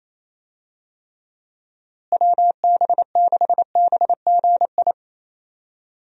Morse code, heard as a keyed tone, about 28 wpm, 700 Hz. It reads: WB6BGI
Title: USA Call Signs - 6 Characters - Repeated 2x - 28wpm